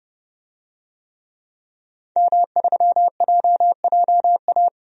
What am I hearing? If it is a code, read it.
Morse code, heard as a keyed tone, about 32 wpm, 700 Hz. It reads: M3JJA
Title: DXCC-100 Call Signs - 5 Characters - 30wpm